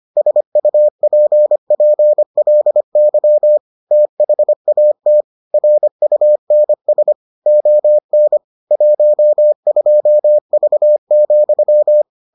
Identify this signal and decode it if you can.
Morse code, heard as a keyed tone, about 25 wpm, 600 Hz. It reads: SUPPLY THAT RUNS ON 12V,